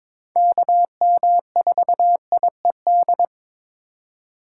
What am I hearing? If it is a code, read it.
Morse code, heard as a keyed tone, about 23 wpm, 700 Hz. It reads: KM4IED
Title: USA Call Signs - 2x Character Spacing - 22wpm